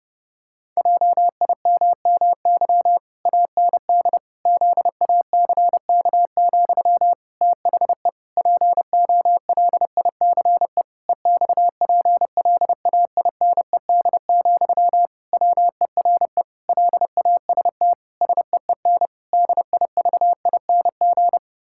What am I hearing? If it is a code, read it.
Morse code, heard as a keyed tone, about 30 wpm, 700 Hz. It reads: JIMMY AND ZACK, THE POLICE EXPLAINED, WERE LAST SEEN DIVING
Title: Pangram Challenge v2 - 30wpm